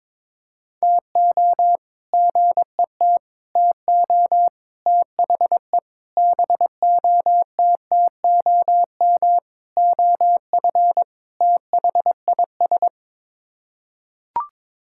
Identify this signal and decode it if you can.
Morse code, heard as a keyed tone, about 21 wpm, 700 Hz. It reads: TO GET TO THE BOTTOM OF THIS
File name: practice / 22wpm / Sets of 7 Words - 22wpm